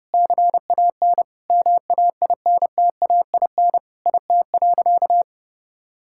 Morse code, heard as a keyed tone, about 30 wpm, 700 Hz. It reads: CAN MAINTAIN IT.
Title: Sentences from Top 1000 Words - Review Part 5 - 30wpm